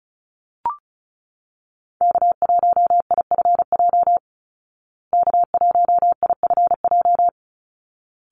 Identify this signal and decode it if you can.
Morse code, heard as a keyed tone, about 34 wpm, 700 Hz. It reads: K1IFJ K1IFJ
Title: Call Signs Repeated 3x - 35wpm